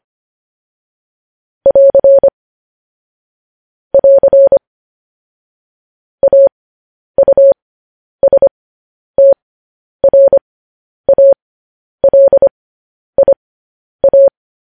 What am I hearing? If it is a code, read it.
Morse code, heard as a keyed tone, about 25 wpm, spaced as for 10 wpm, 550 Hz. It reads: <AR> <AR> AUSTRALIA
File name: podcast / News headlines-1x-25x10